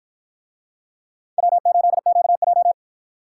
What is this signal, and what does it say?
Morse code, heard as a keyed tone, about 50 wpm, 700 Hz. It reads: W8QJ